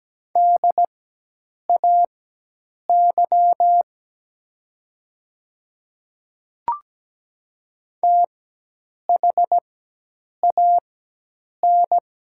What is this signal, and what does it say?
Morse code, heard as a keyed tone, about 17 wpm, spaced as for 8 wpm, 700 Hz. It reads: DAY THAN
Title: 4x Character Spacing - Top 100 Words - 17wpm